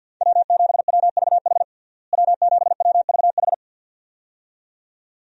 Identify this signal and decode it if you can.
Morse code, heard as a keyed tone, about 46 wpm, 700 Hz. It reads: W7WVH W7WVH
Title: USA Call Signs - 5 Characters - Repeated 2x - 50wpm